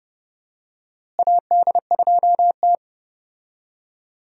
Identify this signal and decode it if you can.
Morse code, heard as a keyed tone, about 30 wpm, 700 Hz. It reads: AD2T
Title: USA Call Signs - 4 Characters - Repeated 3x - 30wpm